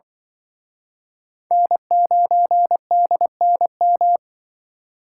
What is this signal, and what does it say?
Morse code, heard as a keyed tone, about 25 wpm, 700 Hz. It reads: N9DNM
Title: Speed Racing - USA Call Signs 20wpm